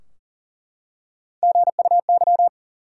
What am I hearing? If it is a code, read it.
Morse code, heard as a keyed tone, about 40 wpm, 700 Hz. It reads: GUY
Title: Three Letter Words - 40wpm